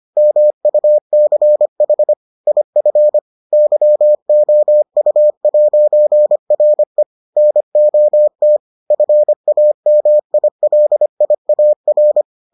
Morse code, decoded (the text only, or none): MUCH IF YOU'RE NOT FAMILIAR